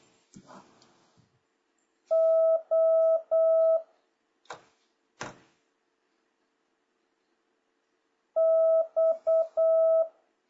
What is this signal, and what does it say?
Morse code, heard as a keyed tone, about 8 wpm, 650 Hz. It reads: O X
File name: jh8cbh / 2026cw